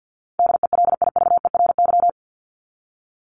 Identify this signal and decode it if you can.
Morse code, heard as a keyed tone, about 50 wpm, 700 Hz. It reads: DELIVERY
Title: Top 1251-1500 Words 50wpm